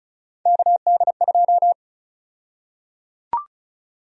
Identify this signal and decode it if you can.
Morse code, heard as a keyed tone, about 34 wpm, 700 Hz. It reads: KD2
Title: USA Call Sign Prefixes - 35wpm